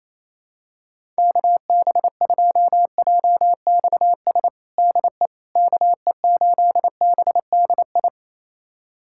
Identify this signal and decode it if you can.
Morse code, heard as a keyed tone, about 28 wpm, 700 Hz. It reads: KB2JXS DE KE8BDI
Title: USA Callsign DE Callsign 28wpm